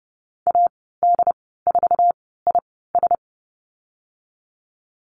Morse code, heard as a keyed tone, about 30 wpm, spaced as for 17 wpm, 700 Hz. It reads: AD4IS
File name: Lesson 14g - Number 2 - Call Signs 30wpm